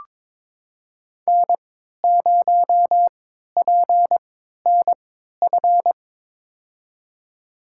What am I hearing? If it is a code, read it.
Morse code, heard as a keyed tone, about 22 wpm, spaced as for 12 wpm, 700 Hz. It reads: N0PNF
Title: USA Call Signs - 5 Characters - 3x Character Spacing - 22wpm